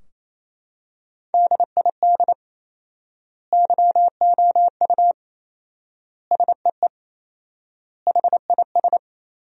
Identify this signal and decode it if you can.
Morse code, heard as a keyed tone, about 27 wpm, 700 Hz. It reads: DID YOU SEE HIS
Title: 4x Word-Spacing - Sentences from Top 300 Words - 28wpm